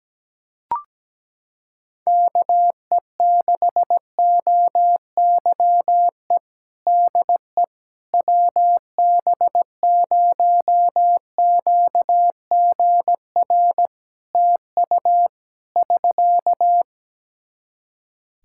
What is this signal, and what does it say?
Morse code, heard as a keyed tone, about 17 wpm, 700 Hz. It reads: KE6OYE DE WB0QGR TU <SK>
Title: Practice QSO - Exchange 4 - Ending - 17wpm